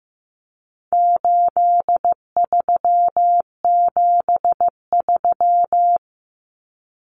Morse code, heard as a keyed tone, about 15 wpm, 700 Hz. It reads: 8373
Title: Numbers - 4-Digit 15wpm